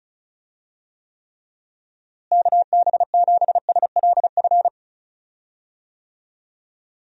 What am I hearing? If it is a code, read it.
Morse code, heard as a keyed tone, about 34 wpm, 700 Hz. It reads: KB7SLF
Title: USA Call Signs - 6 Characters - Repeated 2x - 35wpm